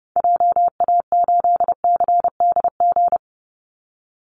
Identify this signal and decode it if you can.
Morse code, heard as a keyed tone, about 28 wpm, 700 Hz. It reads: JA8CDG
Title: Call Signs Repeated 2x - 30wpm